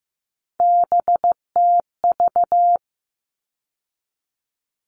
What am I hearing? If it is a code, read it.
Morse code, heard as a keyed tone, about 15 wpm, 700 Hz. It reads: BTV
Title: All Cut Numbers - 3-Digit 15wpm